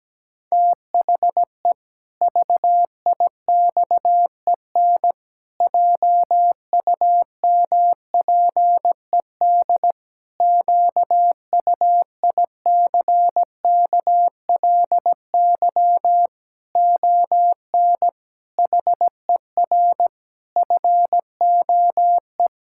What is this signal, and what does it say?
Morse code, heard as a keyed tone, about 17 wpm, 700 Hz. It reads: THE VIXEN JUMPED QUICKLY ON HER FOE
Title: Pangram Challenge v2 - 17wpm